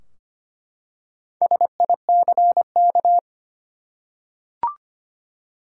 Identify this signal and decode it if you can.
Morse code, heard as a keyed tone, about 25 wpm, 700 Hz. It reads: SICK
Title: Four Letter Words - 25wpm